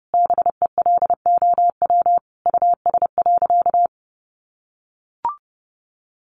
Morse code, heard as a keyed tone, about 28 wpm, 700 Hz. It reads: BELOW US.
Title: Sentences from Top 1400 Words 30wpm